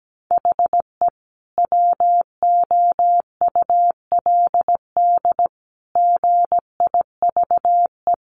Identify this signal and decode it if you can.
Morse code, heard as a keyed tone, about 17 wpm, 700 Hz. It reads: HE WOULD GIVE
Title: Sentences from Top 1200 Words 17wpm